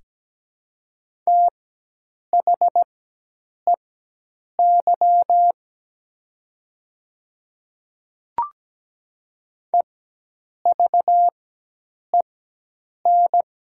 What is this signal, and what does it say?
Morse code, heard as a keyed tone, about 17 wpm, spaced as for 8 wpm, 700 Hz. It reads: THEY EVEN
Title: 4x Character Spacing - Top 200 Words - 17wpm